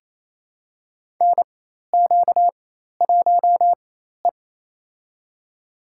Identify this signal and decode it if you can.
Morse code, heard as a keyed tone, about 28 wpm, spaced as for 13 wpm, 700 Hz. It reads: NQ1E
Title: USA Call Signs - 4 Characters - 4x Character Spacing - 28wpm